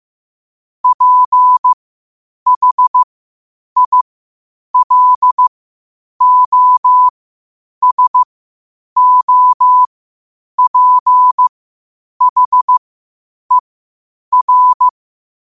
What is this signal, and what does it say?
Morse code, heard as a keyed tone, about 15 wpm, spaced as for 8 wpm, 1000 Hz. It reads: PHILOSOPHER
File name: qotdfiles / 8wpm